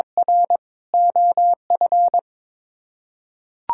Morse code, heard as a keyed tone, about 21 wpm, 700 Hz. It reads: R OF
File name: practice / 22wpm / Sets of 6 Words - 22wpm